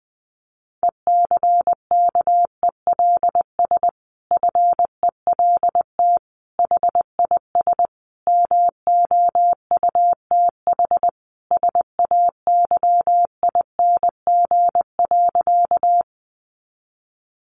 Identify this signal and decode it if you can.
Morse code, heard as a keyed tone, about 20 wpm, 700 Hz. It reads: ECKELS FELT HIS MOUTH SAYING.
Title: A Sound Of Thunder 20wpm